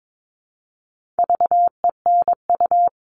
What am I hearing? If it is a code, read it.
Morse code, heard as a keyed tone, about 22 wpm, 700 Hz. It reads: VENU
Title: All Cut Numbers - 4-Digit 22wpm